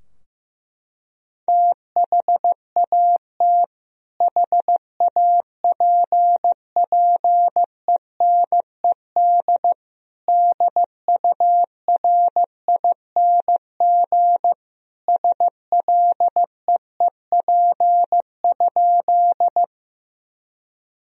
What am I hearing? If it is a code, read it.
Morse code, heard as a keyed tone, about 15 wpm, 700 Hz. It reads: THAT HAPPENED DURING SLEEP?
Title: Sentences from Top 1000 Words - Review Part 2 - 15wpm